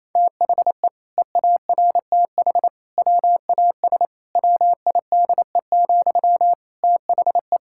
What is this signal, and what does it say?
Morse code, heard as a keyed tone, about 28 wpm, 700 Hz. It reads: THE EARTH WAS WIDE, THE
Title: Sentences from Top 1000 Words - Review Part 6 - 28wpm